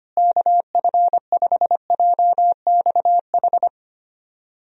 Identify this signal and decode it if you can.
Morse code, heard as a keyed tone, about 25 wpm, 700 Hz. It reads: KF5JXH
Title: USA Call Signs - 6 Characters - Repeated 2x - 25wpm